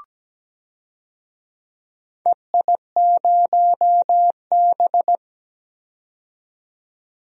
Morse code, heard as a keyed tone, about 17 wpm, 700 Hz. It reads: EI0B
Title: DXCC-100 Call Signs - 4 Characters - 17wpm